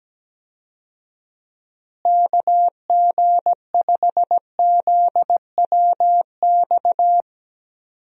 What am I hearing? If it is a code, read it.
Morse code, heard as a keyed tone, about 17 wpm, 700 Hz. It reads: KG5ZWX
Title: USA Call Signs - 6 Characters - Repeated 2x - 17wpm